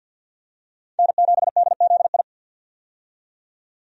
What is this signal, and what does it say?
Morse code, heard as a keyed tone, about 52 wpm, 700 Hz. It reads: N7DZI
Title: USA Call Signs - 5 Characters - 50wpm